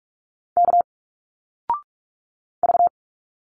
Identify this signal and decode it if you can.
Morse code, heard as a keyed tone, about 45 wpm, 700 Hz. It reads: K V
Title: Single Letters 45wpm